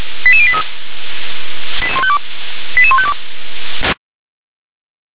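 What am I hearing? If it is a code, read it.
Five-tone selective call (ZVEI), heard as quick runs of five tones, tones of about 70 ms. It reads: 88083 88152
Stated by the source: ZVEI five-tone selective call received by SDR.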